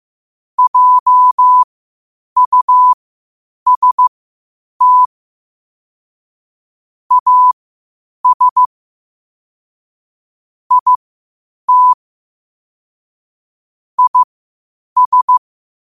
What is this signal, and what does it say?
Morse code, heard as a keyed tone, about 15 wpm, spaced as for 8 wpm, 1000 Hz. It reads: JUST AS IT IS
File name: qotdfiles / 8wpm